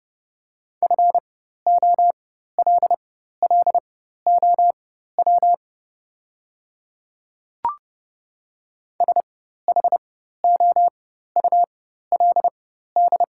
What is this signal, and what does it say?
Morse code, heard as a keyed tone, about 30 wpm, spaced as for 14 wpm, 700 Hz. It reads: FOLLOW SHOULD
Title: 4x Character Spacing - Top 300 Words - 30wpm